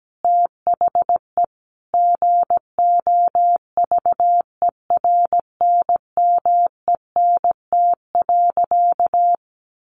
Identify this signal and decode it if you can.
Morse code, heard as a keyed tone, about 17 wpm, 700 Hz. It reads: THE GOVERNMENT.
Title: A Sound Of Thunder 17wpm